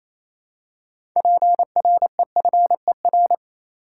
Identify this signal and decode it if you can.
Morse code, heard as a keyed tone, about 28 wpm, 700 Hz. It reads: PREFER
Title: Six Letter Words - 28wpm